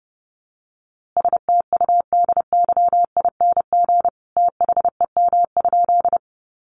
Morse code, heard as a keyed tone, about 30 wpm, 700 Hz. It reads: STUDYING THEM?
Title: A Sound Of Thunder 30wpm